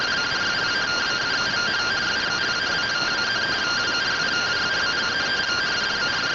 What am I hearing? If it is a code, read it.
Two-tone teleprinter signal, about 170 Hz shift.
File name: findings / SAGITTAL 43mm 4mm.gen-09 rtty rx